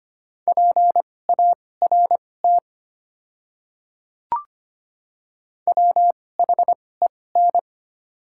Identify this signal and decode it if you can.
Morse code, heard as a keyed tone, about 25 wpm, spaced as for 18 wpm, 700 Hz. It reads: PART WHEN
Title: 2x Character Spacing - Top 500 Words - 25wpm